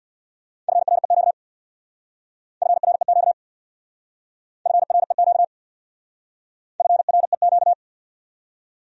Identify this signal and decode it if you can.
Morse code, heard as a keyed tone, about 64 wpm, 700 Hz. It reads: FREQ FREQ FREQ FREQ